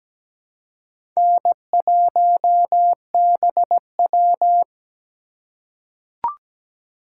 Morse code, heard as a keyed tone, about 17 wpm, 700 Hz. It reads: N1BW
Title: USA Call Signs - 4 Characters - 3x Character Spacing - 17wpm